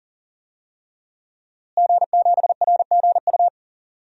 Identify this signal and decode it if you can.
Morse code, heard as a keyed tone, about 40 wpm, 700 Hz. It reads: G7RGU
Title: DXCC-100 Call Signs - 5 Characters - Repeated 3x - 40wpm